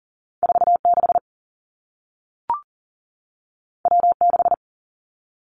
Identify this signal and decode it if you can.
Morse code, heard as a keyed tone, about 40 wpm, 700 Hz. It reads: 4 6 W 6
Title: Two Challenging Letter-Number Combinations 40wpm